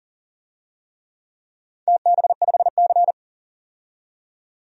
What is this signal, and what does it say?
Morse code, heard as a keyed tone, about 42 wpm, 700 Hz. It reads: TB5C
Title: DXCC-100 Call Signs - 4 Characters - 40wpm